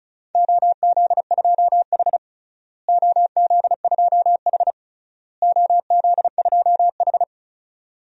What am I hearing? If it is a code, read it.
Morse code, heard as a keyed tone, about 36 wpm, 700 Hz. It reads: OZ2H OZ2H OZ2H
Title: DXCC-100 Call Signs - 4 Characters - Repeated 3x - 35wpm